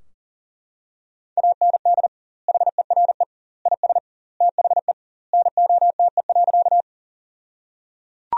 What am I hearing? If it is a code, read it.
Morse code, heard as a keyed tone, about 37 wpm, 700 Hz. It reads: AND HERE IS THE NOTE.
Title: 2x Word-Spacing - Sentences from Top 500 Words - 40wpm